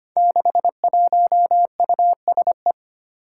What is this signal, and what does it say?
Morse code, heard as a keyed tone, about 25 wpm, 700 Hz. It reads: 61USE
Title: Five Challenging Letter-Number Combinations 25wpm